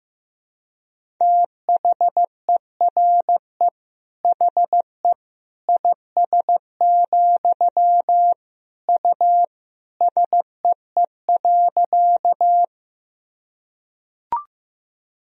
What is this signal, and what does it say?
Morse code, heard as a keyed tone, about 15 wpm, 700 Hz. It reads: THERE HE IS, U SEE.
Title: Sentences from Top 100 Words and Abbreviations 15wpm